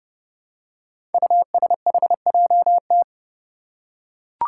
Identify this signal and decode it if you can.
Morse code, heard as a keyed tone, about 30 wpm, 700 Hz. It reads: USHJT
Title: Five Challenging Letter-Number Combinations 30wpm